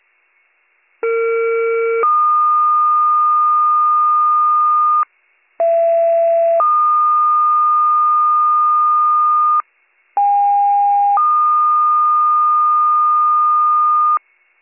Two-tone page, heard as a one-second tone, then a three-second tone, about 460 then 1220 Hz, about 670 then 1220 Hz, about 810 then 1220 Hz. It is heard from 1.0 to 14.2 s.